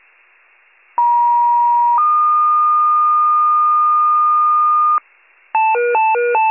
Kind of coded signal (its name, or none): two-tone page